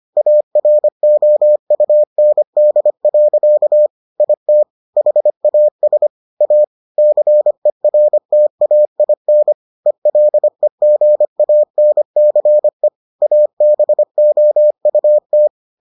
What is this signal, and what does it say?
Morse code, heard as a keyed tone, about 25 wpm, 600 Hz. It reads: AROUND. IT HAS A CERTAIN ELEGANCE ABOUT